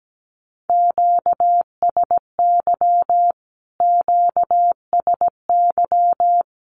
Morse code, heard as a keyed tone, about 17 wpm, 700 Hz. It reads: QSY QSY
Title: Q-Codes Repeated 3x 17wpm